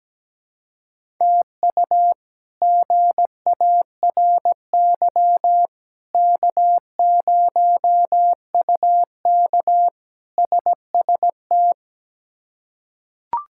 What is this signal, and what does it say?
Morse code, heard as a keyed tone, about 17 wpm, 700 Hz. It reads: TU GARY K0UK SST
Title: K1USN SST QSO Practice - 17wpm